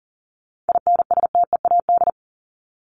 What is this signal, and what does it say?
Morse code, heard as a keyed tone, about 37 wpm, 700 Hz. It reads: INSTEAD
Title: Top 501-750 Words 40wpm